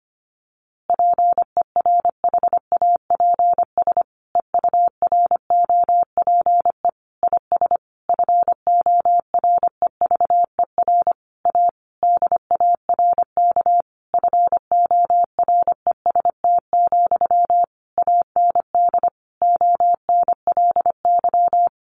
Morse code, heard as a keyed tone, about 25 wpm, 700 Hz. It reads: PERHAPS EUROPE IS FOREVER A DARK FOREST, AND ONLY
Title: A Sound Of Thunder 25wpm